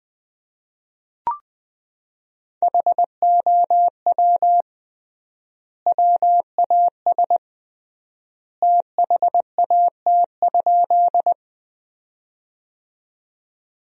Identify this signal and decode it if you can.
Morse code, heard as a keyed tone, about 21 wpm, 700 Hz. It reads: HOW WAS THAT?
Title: 3x Word-Spacing - Sentences from Top 100 Words - 20wpm